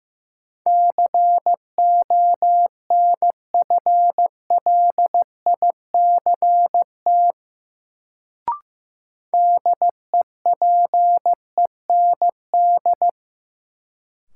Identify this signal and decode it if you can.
Morse code, heard as a keyed tone, about 15 wpm, 700 Hz. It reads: CONFLICT DEPEND
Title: Sentences from Top 1300 Words - New Word Drill Alt - 15wpm